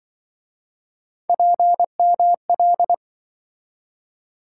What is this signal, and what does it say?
Morse code, heard as a keyed tone, about 26 wpm, 700 Hz. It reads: PML